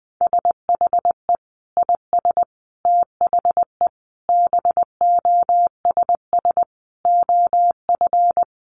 Morse code, heard as a keyed tone, about 19 wpm, 700 Hz. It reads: SHE IS THE BOSS OF
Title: Sentences from Top 700 Words 20wpm